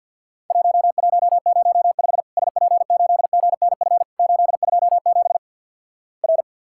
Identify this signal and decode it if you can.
Morse code, heard as a keyed tone, about 51 wpm, 700 Hz. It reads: 1105 SP8GNF 827 R
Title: Worked All Europe DX Contest - QTC Exchange - 50wpm